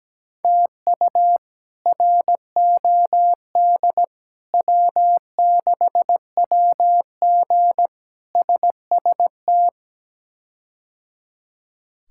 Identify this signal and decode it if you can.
Morse code, heard as a keyed tone, about 17 wpm, 700 Hz. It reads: TU ROD W6WG SST
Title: K1USN SST QSO Practice - 17wpm